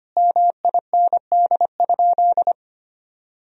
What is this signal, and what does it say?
Morse code, heard as a keyed tone, about 25 wpm, 700 Hz. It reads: MIND?
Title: Sentences from Top 1000 Words - Review Part 4 - 25wpm